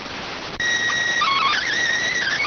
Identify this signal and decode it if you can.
SSTV (slow-scan television), PD 290.